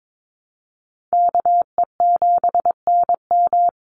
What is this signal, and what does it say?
Morse code, heard as a keyed tone, about 22 wpm, 700 Hz. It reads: KE7NM
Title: USA Call Signs 22wpm